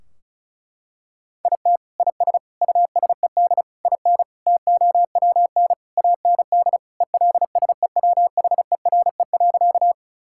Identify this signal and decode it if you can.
Morse code, heard as a keyed tone, about 36 wpm, 700 Hz. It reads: IT IS USED IN TOWN AND ELSEWHERE.
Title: Sentences from Top 1600 Words 35wpm